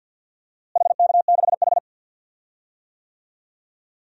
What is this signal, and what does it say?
Morse code, heard as a keyed tone, about 46 wpm, 700 Hz. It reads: HK6H
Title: DXCC-100 Call Signs - 4 Characters - Repeated 2x - 50wpm